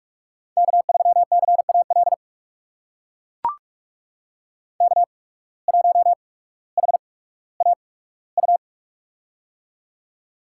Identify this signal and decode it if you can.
Morse code, heard as a keyed tone, about 42 wpm, spaced as for 19 wpm, 700 Hz. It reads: K3CAL K1HAU
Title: USA Call Signs - 5 Characters - 8x Character Spacing - 45wpm